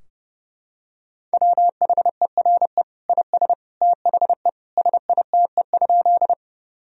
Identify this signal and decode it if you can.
Morse code, heard as a keyed tone, about 32 wpm, 700 Hz. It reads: WHERE IS THE SITE?
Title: Sentences from Top 1000 Words - Review Part 4 - 30wpm